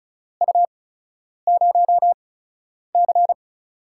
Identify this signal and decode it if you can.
Morse code, heard as a keyed tone, about 36 wpm, spaced as for 10 wpm, 700 Hz. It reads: U0C
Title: USA Call Signs - 8x Character Spacing - 35wpm